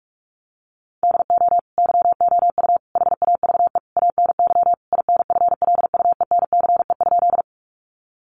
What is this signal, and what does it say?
Morse code, heard as a keyed tone, about 44 wpm, 700 Hz. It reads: DO YOU HAVE ANY INFLUENCE?
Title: Sentences from Top 1100 Words 45wpm